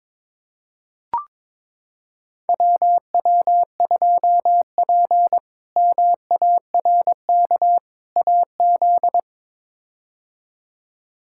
Morse code, heard as a keyed tone, about 22 wpm, 700 Hz. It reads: WW2P MARK AZ
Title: K1USN SST QSO Practice - 22wpm